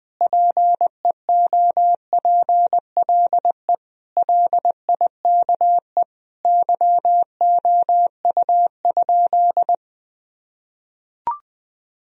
Morse code, heard as a keyed tone, about 21 wpm, 700 Hz. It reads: PEOPLE LIKE YOU?
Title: Sentences from Top 1000 Words - Review Part 1 - 20wpm